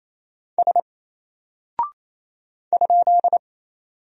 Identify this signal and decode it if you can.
Morse code, heard as a keyed tone, about 28 wpm, 700 Hz. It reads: S ?
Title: Single Character (Letter:Number:Punctuation:Prosign) Alt 28wpm